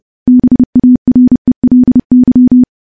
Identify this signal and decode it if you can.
Morse code, heard as a keyed tone, about 30 wpm, 250 Hz. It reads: BARELY
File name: morse / morse-training-30WPM